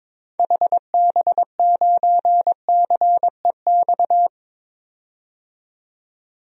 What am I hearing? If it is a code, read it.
Morse code, heard as a keyed tone, about 22 wpm, 700 Hz. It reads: HB9CEX
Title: DXCC-100 Call Signs - 6 Characters - 22wpm